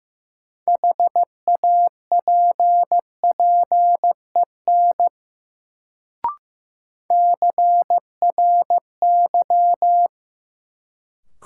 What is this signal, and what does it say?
Morse code, heard as a keyed tone, about 15 wpm, 700 Hz. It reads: HAPPEN CRY
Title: Sets of 5 Words Encore - New Word Drill - 15wpm